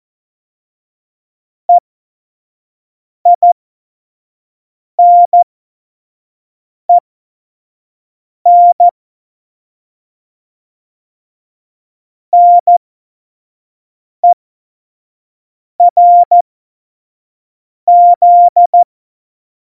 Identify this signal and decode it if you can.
Morse code, heard as a keyed tone, about 14 wpm, spaced as for 5 wpm, 700 Hz. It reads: EINEN NERZ
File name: Morsekurs / Witze / Witz 13. Oktober 2020, 25 F70